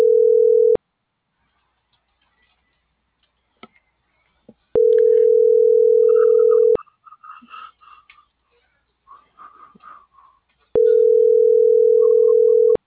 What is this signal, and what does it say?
Ringback tone (North America), 440 + 480 Hz.